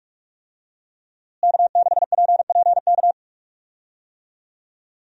Morse code, heard as a keyed tone, about 46 wpm, 700 Hz. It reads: K6PPK
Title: USA Call Signs - 5 Characters - Repeated 3x - 45wpm